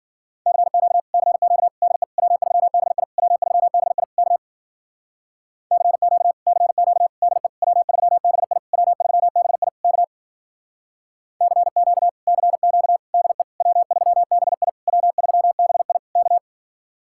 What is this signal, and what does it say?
Morse code, heard as a keyed tone, about 51 wpm, 700 Hz. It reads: CQ CQ DE W3BI W3BI K CQ CQ DE W3BI W3BI K CQ CQ DE W3BI W3BI K